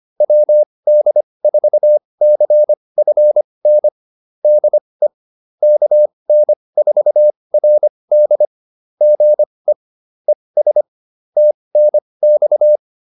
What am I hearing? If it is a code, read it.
Morse code, heard as a keyed tone, about 25 wpm, spaced as for 20 wpm, 600 Hz. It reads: WD4CFN DE KN4RD GE ES TNX